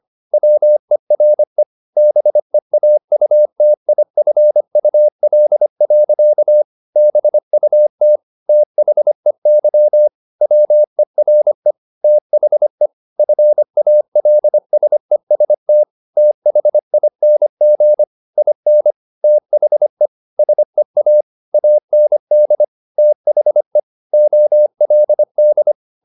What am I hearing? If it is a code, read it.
Morse code, heard as a keyed tone, about 25 wpm, 600 Hz. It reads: WERE BEAUTIFUL. BUT THEY WERE THE FALSEST THING IN THE SEA AND THE OLD
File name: the old man and the sea part 1 25wpm